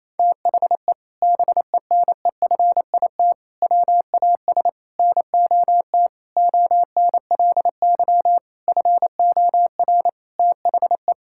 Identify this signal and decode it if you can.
Morse code, heard as a keyed tone, about 28 wpm, 700 Hz. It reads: THE BENEFIT WAS NOT ONLY FOR THE
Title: Sentences from Top 1000 Words - Review Part 6 - 28wpm